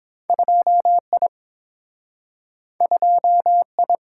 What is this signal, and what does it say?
Morse code, heard as a keyed tone, about 30 wpm, 700 Hz. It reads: 2I 2I